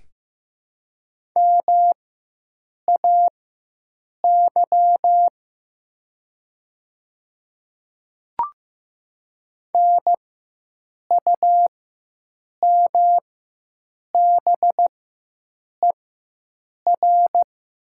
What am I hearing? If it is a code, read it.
Morse code, heard as a keyed tone, about 15 wpm, spaced as for 7 wpm, 700 Hz. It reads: MAY NUMBER
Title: 4x Character Spacing - Top 100 Words - 15wpm